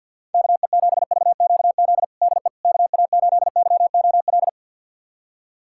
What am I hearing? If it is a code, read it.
Morse code, heard as a keyed tone, about 48 wpm, 700 Hz. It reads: KE7VQZ DE KA8YOL
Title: USA Callsign DE Callsign - 6 Characters - 50wpm